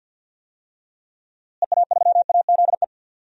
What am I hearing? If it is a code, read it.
Morse code, heard as a keyed tone, about 47 wpm, 700 Hz. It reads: EA3AZE
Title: DXCC-100 Call Signs - 6 Characters - Repeated 3x - 50wpm